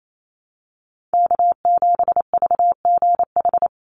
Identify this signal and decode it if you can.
Morse code, heard as a keyed tone, about 28 wpm, 700 Hz. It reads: K7VGH
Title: USA Call Signs 28wpm